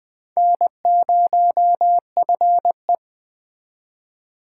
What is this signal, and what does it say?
Morse code, heard as a keyed tone, about 21 wpm, 700 Hz. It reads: N0FE
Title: Speed Racing - USA 1x2 Call Signs 17wpm